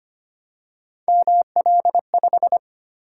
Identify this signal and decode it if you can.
Morse code, heard as a keyed tone, about 25 wpm, 700 Hz. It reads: ML5
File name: Three Letter-Number Combinations - 25wpm